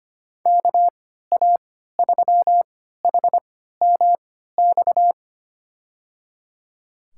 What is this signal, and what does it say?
Morse code, heard as a keyed tone, about 25 wpm, spaced as for 14 wpm, 700 Hz. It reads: KA3HMX
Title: USA Call Signs - 6 Characters - 3x Character Spacing - 25wpm